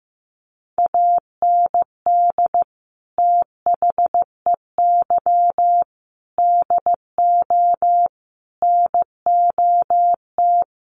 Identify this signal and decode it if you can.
Morse code, heard as a keyed tone, about 15 wpm, 700 Hz. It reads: AND THEY DO NOT
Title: Sets of 4 Words 15wpm